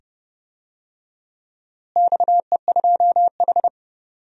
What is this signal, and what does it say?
Morse code, heard as a keyed tone, about 30 wpm, 700 Hz. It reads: XE2H
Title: DXCC-100 Call Signs - 4 Characters - Repeated 2x - 30wpm